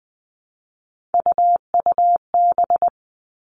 Morse code, heard as a keyed tone, about 20 wpm, 700 Hz. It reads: UUB
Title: Three Challenging Letter-Number Combinations 20wpm